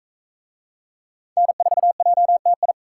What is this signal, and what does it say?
Morse code, heard as a keyed tone, about 41 wpm, 700 Hz. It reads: N4JTI